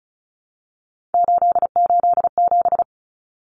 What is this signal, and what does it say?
Morse code, heard as a keyed tone, about 34 wpm, 700 Hz. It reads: 887
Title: Common Cut Numbers - 3-Digit 35wpm